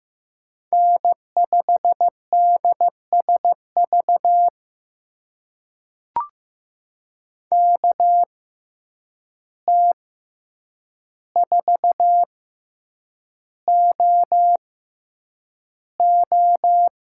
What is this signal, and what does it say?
Morse code, heard as a keyed tone, about 15 wpm, spaced as for 8 wpm, 700 Hz. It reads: N5DSV KT4OO
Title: USA Call Signs - 5 Characters - 6x Character Spacing - 15wpm